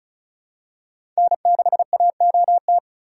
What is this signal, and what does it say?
Morse code, heard as a keyed tone, about 36 wpm, 700 Hz. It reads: N6AOT
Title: USA Call Signs - 5 Characters - 3x Character Spacing - 35wpm